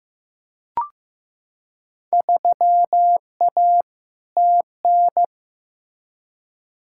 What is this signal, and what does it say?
Morse code, heard as a keyed tone, about 15 wpm, 700 Hz. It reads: 3A TN
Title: Field Day Exchanges - 15wpm